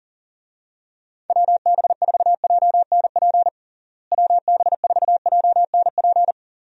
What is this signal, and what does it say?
Morse code, heard as a keyed tone, about 40 wpm, 700 Hz. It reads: WB4JNP WB4JNP
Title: USA Call Signs - 6 Characters - Repeated 3x - 40wpm